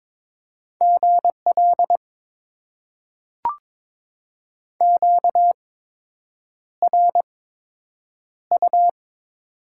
Morse code, heard as a keyed tone, about 22 wpm, spaced as for 8 wpm, 700 Hz. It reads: GL QRU
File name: Top 90 QSO Elements - 8x Character Spacing - 22wpm